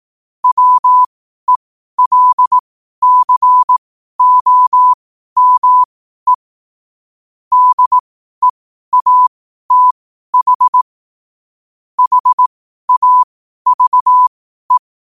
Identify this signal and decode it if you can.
Morse code, heard as a keyed tone, about 18 wpm, spaced as for 12 wpm, 1000 Hz. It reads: WELCOME DEATH HAVE
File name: qotdfiles / 12wpm